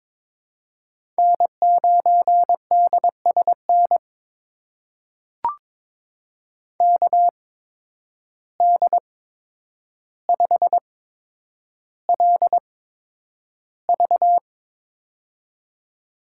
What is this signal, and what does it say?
Morse code, heard as a keyed tone, about 22 wpm, spaced as for 9 wpm, 700 Hz. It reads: N9DSN KD5LV
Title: USA Call Signs - 5 Characters - 8x Character Spacing - 22wpm